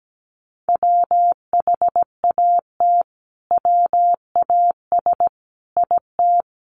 Morse code, heard as a keyed tone, about 17 wpm, 700 Hz. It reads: WHAT WAS IT
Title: Sentences from Top 100 Words 17wpm